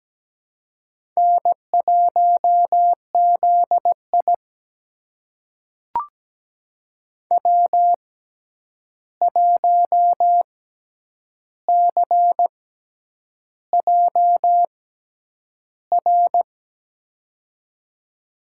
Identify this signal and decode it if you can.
Morse code, heard as a keyed tone, about 17 wpm, spaced as for 8 wpm, 700 Hz. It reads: N1ZI W1CJR
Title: USA Call Signs - 6x Character Spacing - 17wpm